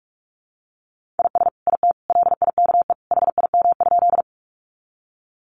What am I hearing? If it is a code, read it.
Morse code, heard as a keyed tone, about 44 wpm, 700 Hz. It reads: IS IT LIKE HIM?
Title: Sentences from Top 100 Words 45wpm